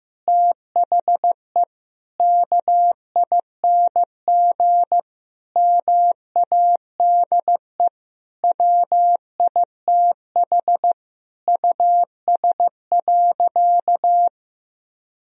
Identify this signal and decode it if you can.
Morse code, heard as a keyed tone, about 16 wpm, 700 Hz. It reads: THE KING MADE WITH US.
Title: Sentences from Top 1000 Words - Review Part 4 - 15wpm